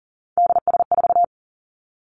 Morse code, heard as a keyed tone, about 40 wpm, 700 Hz. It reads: DS4
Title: Three Challenging Letter-Number Combinations 40wpm